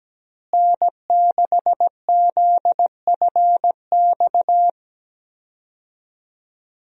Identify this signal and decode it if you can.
Morse code, heard as a keyed tone, about 17 wpm, 700 Hz. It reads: N6ZFX
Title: USA Call Signs - 5 Characters - Repeated 3x - 17wpm